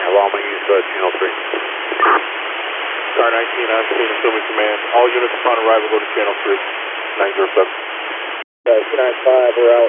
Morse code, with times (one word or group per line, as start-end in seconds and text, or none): none